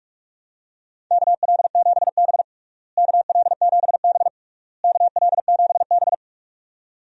Morse code, heard as a keyed tone, about 46 wpm, 700 Hz. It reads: KL7B KL7B KL7B
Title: DXCC-100 Call Signs - 4 Characters - Repeated 3x - 45wpm